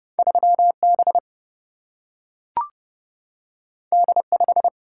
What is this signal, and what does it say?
Morse code, heard as a keyed tone, about 30 wpm, 700 Hz. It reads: 3B D5
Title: Single-Letter Single-Number 30wpm